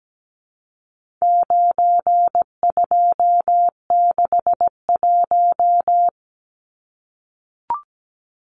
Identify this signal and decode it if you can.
Morse code, heard as a keyed tone, about 17 wpm, 700 Hz. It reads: 9261
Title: Numbers - 4-Digit 17wpm